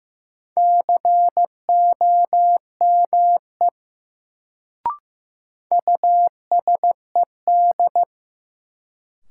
Morse code, heard as a keyed tone, about 15 wpm, 700 Hz. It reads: COME USED
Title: Sets of 2 Words - New Word Drill - 15wpm